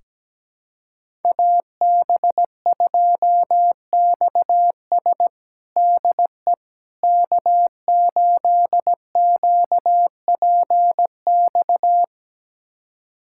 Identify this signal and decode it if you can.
Morse code, heard as a keyed tone, about 17 wpm, 700 Hz. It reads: AB2XS DE K8QPX
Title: USA Callsign DE Callsign - 5 Characters - 17wpm